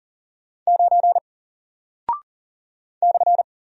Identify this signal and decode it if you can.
Morse code, heard as a keyed tone, about 40 wpm, 700 Hz. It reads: 9 /
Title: Single Character (Letter:Number:Punctuation:Prosign) Alt 40wpm